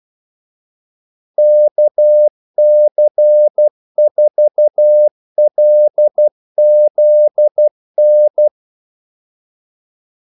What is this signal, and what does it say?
Morse code, heard as a keyed tone, about 12 wpm, 600 Hz. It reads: KC4LZN
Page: 2 ekstra ordmellemrum mellem kaldesignalerne